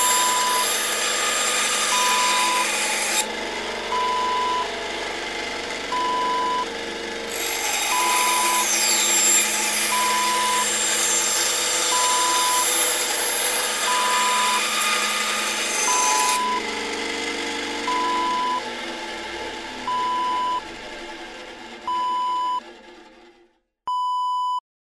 There is no code.